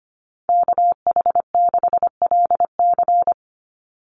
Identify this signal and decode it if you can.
Morse code, heard as a keyed tone, about 25 wpm, 700 Hz. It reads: KH6LC
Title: Call Signs Repeated 3x - 25wpm